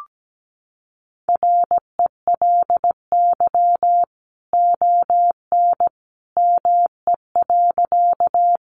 Morse code, heard as a keyed tone, about 17 wpm, 700 Hz. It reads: RELY ON ME.
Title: Sentences from Top 1200 Words 17wpm